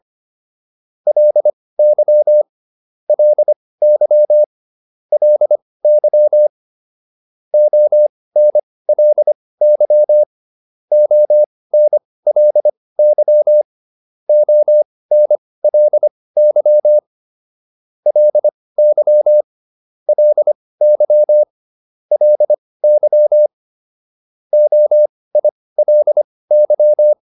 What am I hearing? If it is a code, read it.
Morse code, heard as a keyed tone, about 25 wpm, spaced as for 18 wpm, 600 Hz. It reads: LY LY LY ONLY ONLY ONLY LY LY LY OILY